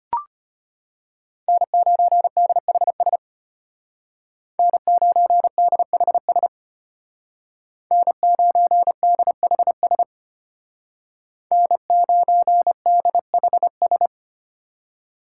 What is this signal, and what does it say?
Morse code, heard as a keyed tone, about 29 wpm, 700 Hz. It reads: N9DHS N9DHS N9DHS N9DHS
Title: Speed Racing - USA Call Signs 25wpm